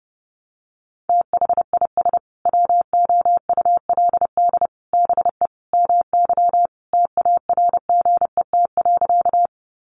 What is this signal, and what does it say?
Morse code, heard as a keyed tone, about 30 wpm, 700 Hz. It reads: THIS WOULD BE MY TARGET.
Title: Sentences from Top 1200 Words 30wpm